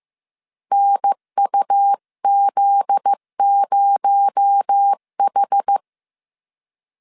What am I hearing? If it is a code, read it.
Morse code, heard as a keyed tone, about 15 wpm, 800 Hz. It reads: NUZ0H
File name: CW / DET / 1 9 1500C S A 14wpm 3222